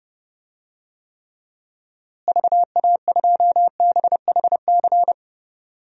Morse code, heard as a keyed tone, about 30 wpm, 700 Hz. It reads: VA2BHC
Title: DXCC-100 Call Signs - 6 Characters - Repeated 2x - 30wpm